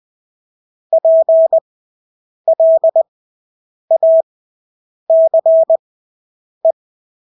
Morse code, heard as a keyed tone, about 20 wpm, spaced as for 8 wpm, 650 Hz. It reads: PLACE
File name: Engelske ord 20/8 wpm